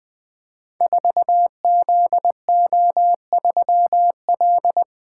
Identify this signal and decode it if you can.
Morse code, heard as a keyed tone, about 21 wpm, 700 Hz. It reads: 4ZO3L
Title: Code Groups - 5 Letter:Numbers - 20wpm